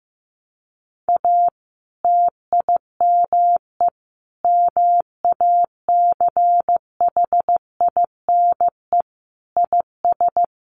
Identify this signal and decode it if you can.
Morse code, heard as a keyed tone, about 15 wpm, 700 Hz. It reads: A TIME MACHINE IS
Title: A Sound Of Thunder 15wpm